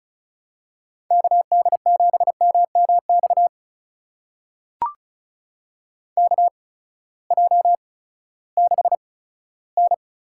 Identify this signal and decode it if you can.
Morse code, heard as a keyed tone, about 36 wpm, spaced as for 18 wpm, 700 Hz. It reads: KD7MMX KJ6N
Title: USA Call Signs - 6 Characters - 8x Character Spacing - 35wpm